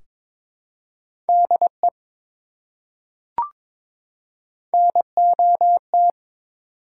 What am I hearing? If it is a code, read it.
Morse code, heard as a keyed tone, about 22 wpm, 700 Hz. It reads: DE NOT
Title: Top 180 QSO Elements 22wpm